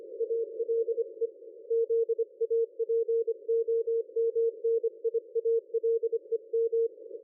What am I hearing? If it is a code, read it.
Morse code, heard as a keyed tone, about 25 wpm, 460 Hz. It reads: ALE ZAPOMNIALEM
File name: dlazaawansowanych / nagraniazpasma / ExpertSDR2 line out 23.01.2020 14.12.00